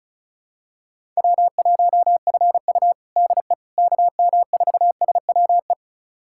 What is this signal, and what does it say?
Morse code, heard as a keyed tone, about 34 wpm, 700 Hz. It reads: W1FU DE KM4SWE
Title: USA Callsign DE Callsign 35wpm